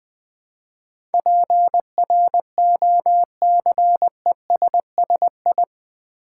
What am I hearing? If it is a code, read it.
Morse code, heard as a keyed tone, about 20 wpm, 700 Hz. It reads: PROCESSI
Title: Build A Word - 4 Letters At A Time - 10-Letter Words - 20wpm